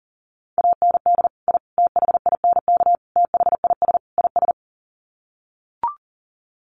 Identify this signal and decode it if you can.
Morse code, heard as a keyed tone, about 40 wpm, 700 Hz. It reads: AND I THINK THIS IS
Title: Sets of 5 Words Encore 40wpm